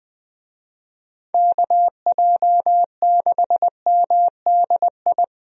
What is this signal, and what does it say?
Morse code, heard as a keyed tone, about 20 wpm, 700 Hz. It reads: KJ6MDI
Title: USA Call Signs - 6 Characters - 20wpm